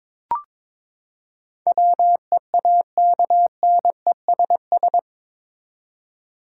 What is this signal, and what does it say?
Morse code, heard as a keyed tone, about 22 wpm, 700 Hz. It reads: WEAKNESS
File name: practice / 22wpm / Eight Letter Words - 22wpm